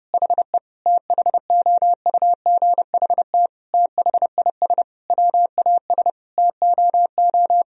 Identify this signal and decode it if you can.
Morse code, heard as a keyed tone, about 30 wpm, 700 Hz. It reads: HE THOUGHT THIS WAS TOO
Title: Sentences from Top 1000 Words - Review Part 4 - 30wpm